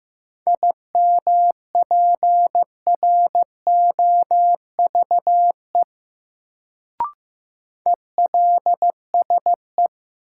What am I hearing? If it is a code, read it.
Morse code, heard as a keyed tone, about 16 wpm, 700 Hz. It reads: IMPROVE ELSE
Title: Sentences from Top 800 Words - New Word Drill - 15wpm